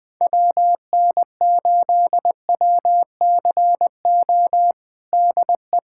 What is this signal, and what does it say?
Morse code, heard as a keyed tone, about 21 wpm, 700 Hz. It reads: WN8WCO DE
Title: USA Callsign DE Callsign - 6 Characters - 20wpm